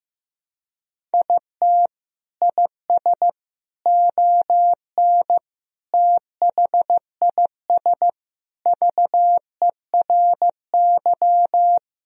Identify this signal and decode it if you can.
Morse code, heard as a keyed tone, about 15 wpm, 700 Hz. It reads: IT IS ON THIS VERY
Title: Sentences from Top 1000 Words - Review Part 1 - 15wpm